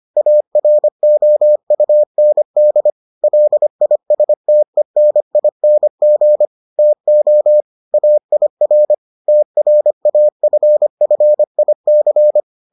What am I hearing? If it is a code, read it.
Morse code, heard as a keyed tone, about 25 wpm, 600 Hz. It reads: AROUND LISTENING TO AIR TRAFFIC